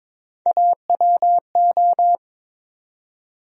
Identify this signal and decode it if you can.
Morse code, heard as a keyed tone, about 22 wpm, 700 Hz. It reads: AWO
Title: USA Call Sign Postfixes - 3 Letters 22wpm